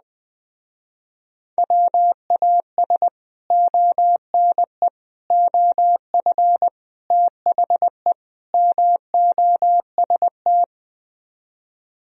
Morse code, heard as a keyed tone, about 20 wpm, 700 Hz. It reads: WAS ONE OF THE MOST